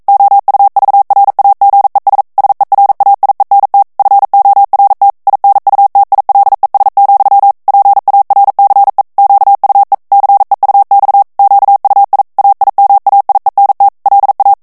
Morse code, heard as a keyed tone, about 42 wpm, 800 Hz. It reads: OUVRAGES SERAIENT FORT INUTILES, PARCE QUE CEUX QUI AIMAIENT LA
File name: CW / FICHIERS-MP3 / FRANCAIS TEXTE CLAIR 2 VITESSE 40 MOTS MINUTES